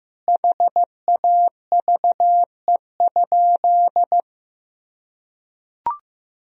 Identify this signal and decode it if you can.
Morse code, heard as a keyed tone, about 15 wpm, 700 Hz. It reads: HAVE?
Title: Sentences from Top 1000 Words - Review Part 1 - 15wpm